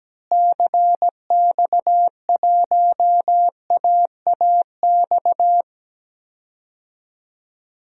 Morse code, heard as a keyed tone, about 17 wpm, 700 Hz. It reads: CX1AAX
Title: DXCC-100 Call Signs - 6 Characters - Repeated 3x - 17wpm